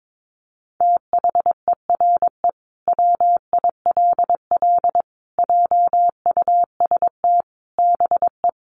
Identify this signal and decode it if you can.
Morse code, heard as a keyed tone, about 22 wpm, 700 Hz. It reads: THERE WILL JUST BE
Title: Sentences from Top 200 Words 22wpm